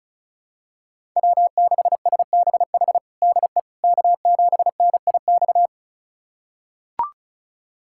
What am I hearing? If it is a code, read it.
Morse code, heard as a keyed tone, about 34 wpm, 700 Hz. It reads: W6SBH DE K7NIX
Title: USA Callsign DE Callsign 35wpm